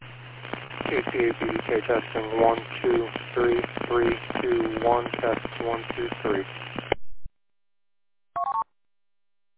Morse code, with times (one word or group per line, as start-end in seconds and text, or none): none